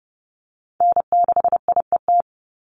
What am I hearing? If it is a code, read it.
Morse code, heard as a keyed tone, about 30 wpm, 700 Hz. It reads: N6IET
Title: Call Signs Repeated 2x - 30wpm